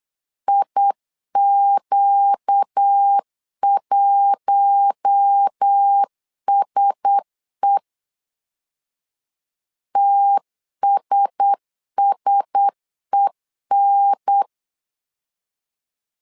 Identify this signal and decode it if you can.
Morse code, heard as a keyed tone, about 8 wpm, 800 Hz. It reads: IQ1SE TSSEN
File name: LEZIONI / 4 Q1S 8wpm 2710